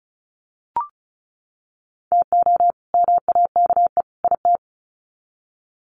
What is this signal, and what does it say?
Morse code, heard as a keyed tone, about 34 wpm, 700 Hz. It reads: TO MAKE IT
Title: Sets of 3 Words 35wpm